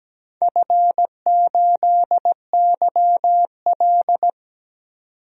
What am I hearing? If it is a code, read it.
Morse code, heard as a keyed tone, about 17 wpm, 700 Hz. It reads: F8YL
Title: DXCC-100 Call Signs - 4 Characters - 17wpm